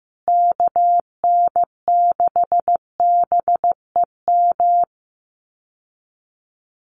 Morse code, heard as a keyed tone, about 15 wpm, 700 Hz. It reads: KN6BEM
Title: USA Call Signs Repeated 3x 15wpm